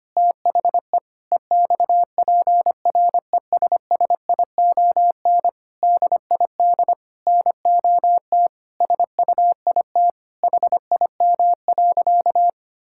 Morse code, heard as a keyed tone, about 25 wpm, 700 Hz. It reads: THE EXPRESSION DID NOT SUIT HIM.
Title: Sentences from Top 1600 Words 25wpm